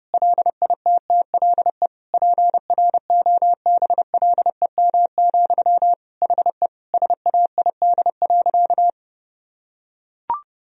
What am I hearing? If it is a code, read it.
Morse code, heard as a keyed tone, about 30 wpm, 700 Hz. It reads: LITTLE PROBLEM, HE SAID.
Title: Sentences from Top 1000 Words - Review Part 2 - 30wpm